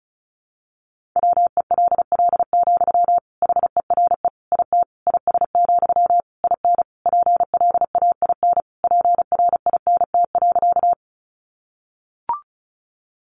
Morse code, heard as a keyed tone, about 34 wpm, 700 Hz. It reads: WELL, HERE IT IS, IN PLAIN PRINT.
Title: Sentences from Top 1400 Words 35wpm